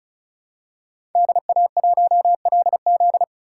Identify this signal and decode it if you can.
Morse code, heard as a keyed tone, about 34 wpm, 700 Hz. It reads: DA1LZ
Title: USA Call Signs - 5 Characters - 3x Character Spacing - 35wpm